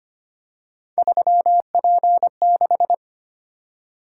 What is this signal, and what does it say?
Morse code, heard as a keyed tone, about 29 wpm, 700 Hz. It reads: 3P6
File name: Speed-Racing - Three Letter-Number Combinations - 25wpm